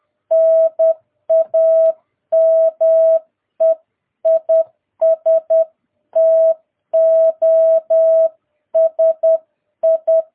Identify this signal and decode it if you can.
Morse code, heard as a keyed tone, about 10 wpm, 650 Hz. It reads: NAME IS TOSI
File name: jh8cbh / 2022cw